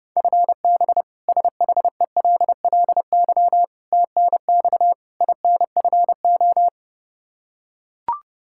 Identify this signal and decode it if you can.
Morse code, heard as a keyed tone, about 30 wpm, 700 Hz. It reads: FB SHELLY TNX INFO
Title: K1USN SST QSO Practice 30wpm